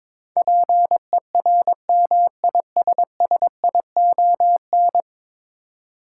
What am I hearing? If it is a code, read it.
Morse code, heard as a keyed tone, about 22 wpm, 700 Hz. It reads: PERMISSION
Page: morse-code-ninja-practice-22wpm